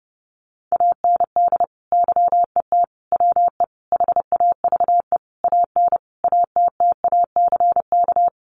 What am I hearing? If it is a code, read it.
Morse code, heard as a keyed tone, about 28 wpm, 700 Hz. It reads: AND YET WE HAVE AN ATTACK
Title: Sentences from Top 1500 Words 30wpm